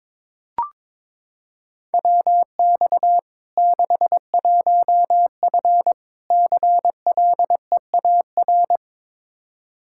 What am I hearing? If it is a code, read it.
Morse code, heard as a keyed tone, about 22 wpm, 700 Hz. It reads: WX 61F CLEAR
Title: K1USN SST QSO Practice 22wpm